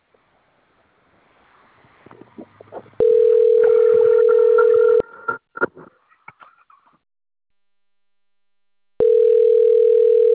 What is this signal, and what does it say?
Ringback tone (North America), 440 + 480 Hz.